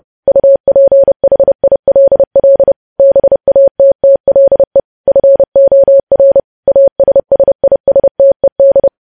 Morse code, heard as a keyed tone, about 30 wpm, 550 Hz. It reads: UPHILL BATTLE FOR ASSISTED
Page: news headlines in morse code at 30 wpm